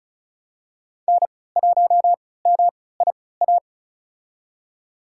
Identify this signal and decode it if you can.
Morse code, heard as a keyed tone, about 34 wpm, spaced as for 20 wpm, 700 Hz. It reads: N1MIA
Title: USA Call Signs - 5 Characters - 3x Character Spacing - 35wpm